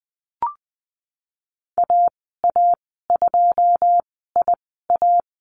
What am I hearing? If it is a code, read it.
Morse code, heard as a keyed tone, about 20 wpm, spaced as for 15 wpm, 700 Hz. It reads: AA2IA
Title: 088 Lesson 15g - Number 5 - Call Signs 20wpm